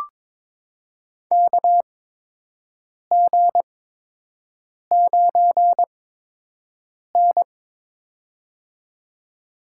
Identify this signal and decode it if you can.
Morse code, heard as a keyed tone, about 23 wpm, spaced as for 6 wpm, 700 Hz. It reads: KG9N
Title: USA Call Signs - 4 Characters - 8x Character Spacing - 22wpm